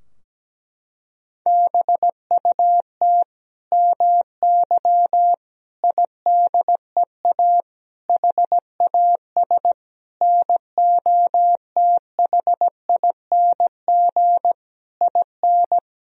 Morse code, heard as a keyed tone, about 17 wpm, 700 Hz. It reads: BUT MY IDEA HAS NOTHING IN
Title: Sentences from Top 1000 Words - Review Part 3 - 17wpm